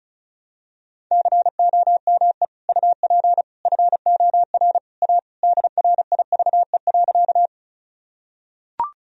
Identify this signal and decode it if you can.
Morse code, heard as a keyed tone, about 36 wpm, 700 Hz. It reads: COME UP FOR A DRIVE.
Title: Sentences from Top 1000 Words - Review Part 2 - 35wpm